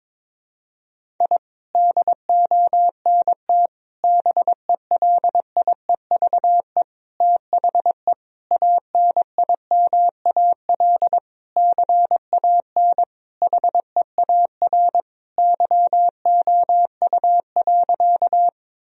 Morse code, heard as a keyed tone, about 22 wpm, 700 Hz. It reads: I DONT BELIEVE THE ANIMAL CAN HEAR YOU.
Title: Sentences from Top 1000 Words - Review Part 1 - 22wpm